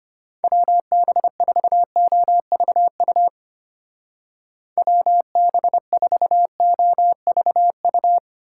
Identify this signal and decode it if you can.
Morse code, heard as a keyed tone, about 28 wpm, 700 Hz. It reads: WB4OVU WB4OVU
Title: Speed Racing - USA 2x3 Call Signs 25wpm